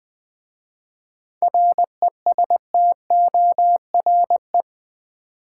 Morse code, heard as a keyed tone, about 20 wpm, 700 Hz. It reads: RESTORE
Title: 7 Letter Words - First 3 Letters Given - 20wpm